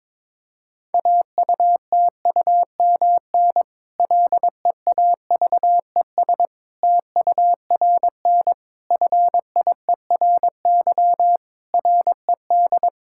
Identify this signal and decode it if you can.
Morse code, heard as a keyed tone, about 22 wpm, 700 Hz. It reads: AUTUMN LEAVES TURN FIERY RED
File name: Difficult Sentences with Hints - 22wpm